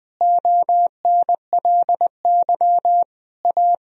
Morse code, heard as a keyed tone, about 20 wpm, 700 Hz. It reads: ONLY A
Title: Sentences from Top 1000 Words - Review Part 1 - 20wpm